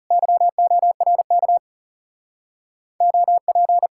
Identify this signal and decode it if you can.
Morse code, heard as a keyed tone, about 40 wpm, 700 Hz. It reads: YORK OP